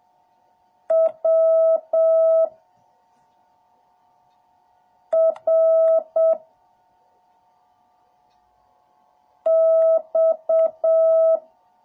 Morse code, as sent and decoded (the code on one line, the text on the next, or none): .-- / .-. / -..-
W R X